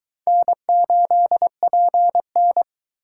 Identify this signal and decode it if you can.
Morse code, heard as a keyed tone, about 21 wpm, 700 Hz. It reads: N8PN